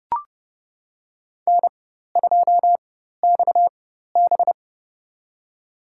Morse code, heard as a keyed tone, about 30 wpm, spaced as for 14 wpm, 700 Hz. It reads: N2XB
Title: USA Call Signs - 4 Characters - 4x Character Spacing - 30wpm